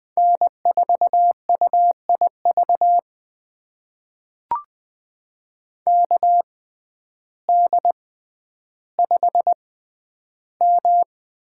Morse code, heard as a keyed tone, about 20 wpm, spaced as for 11 wpm, 700 Hz. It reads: N4UIV KD5M
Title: USA Call Signs - 5 Characters - 6x Character Spacing - 20wpm